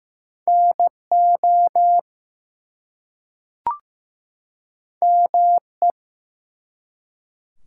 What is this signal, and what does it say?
Morse code, heard as a keyed tone, about 15 wpm, 700 Hz. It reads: NO ME
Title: Two Letter Words - 15wpm